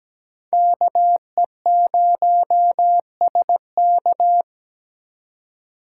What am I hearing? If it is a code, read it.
Morse code, heard as a keyed tone, about 20 wpm, 700 Hz. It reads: KE0SK
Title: Speed Racing - USA 2x2 Call Signs 17wpm